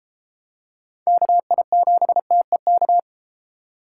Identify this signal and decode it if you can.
Morse code, heard as a keyed tone, about 32 wpm, 700 Hz. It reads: KI7TEK